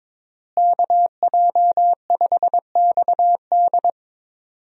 Morse code, heard as a keyed tone, about 22 wpm, 700 Hz. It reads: KJ5XD
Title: USA Call Signs - 5 Characters - Repeated 2x - 22wpm